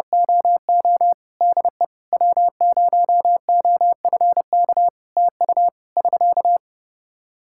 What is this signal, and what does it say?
Morse code, heard as a keyed tone, about 30 wpm, 700 Hz. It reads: OO DE W0OFK TU <SK>
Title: K1USN SST QSO Practice 30wpm